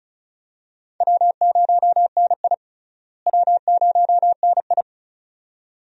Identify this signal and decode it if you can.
Morse code, heard as a keyed tone, about 34 wpm, 700 Hz. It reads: W0NI W0NI
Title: USA Call Signs - 4 Characters - Repeated 2x - 35wpm